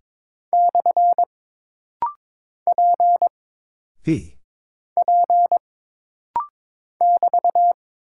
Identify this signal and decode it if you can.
Morse code, heard as a keyed tone, about 22 wpm, 700 Hz. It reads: / P P <BT>
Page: This rapid-fire timing is geared to push your ICR (Instant Character Recognition) skills.